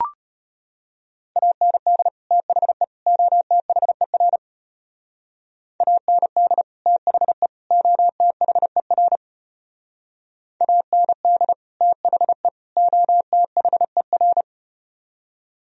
Morse code, heard as a keyed tone, about 33 wpm, 700 Hz. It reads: AND THE OTHER AND THE OTHER AND THE OTHER